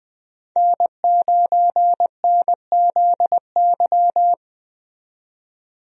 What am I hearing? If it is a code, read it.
Morse code, heard as a keyed tone, about 21 wpm, 700 Hz. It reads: N9NZY
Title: USA Call Signs - 5 Characters - 20wpm